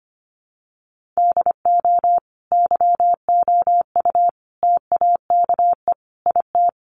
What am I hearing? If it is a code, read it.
Morse code, heard as a keyed tone, about 25 wpm, 700 Hz. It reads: DO YOU TAKE IT
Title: Sentences from Top 1500 Words 25wpm